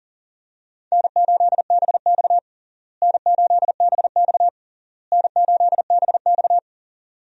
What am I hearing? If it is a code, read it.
Morse code, heard as a keyed tone, about 40 wpm, 700 Hz. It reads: N8BX N8BX N8BX
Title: USA Call Signs - 4 Characters - Repeated 3x - 40wpm